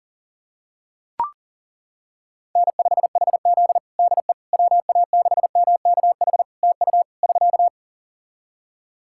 Morse code, heard as a keyed tone, about 40 wpm, 700 Hz. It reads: N5HZ DE WA6MKH TU <SK>
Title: Practice QSO - Exchange 4 - Ending - 40wpm